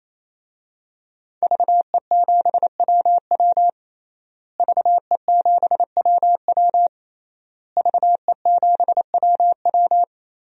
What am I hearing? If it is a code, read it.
Morse code, heard as a keyed tone, about 27 wpm, 700 Hz. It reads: VE7WW VE7WW VE7WW
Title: DXCC-100 Call Signs - 5 Characters - Repeated 3x - 28wpm